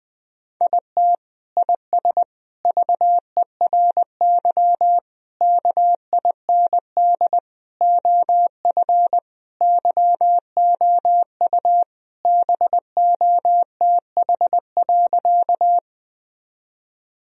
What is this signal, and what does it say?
Morse code, heard as a keyed tone, about 21 wpm, 700 Hz. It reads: IT IS VERY KIND OF YOU BOTH.
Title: Sentences from Top 1000 Words - Review Part 1 - 20wpm